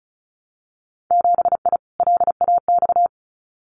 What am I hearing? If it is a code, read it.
Morse code, heard as a keyed tone, about 36 wpm, 700 Hz. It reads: 7I LAX
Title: Winter Field Day Exchanges Encore 35wpm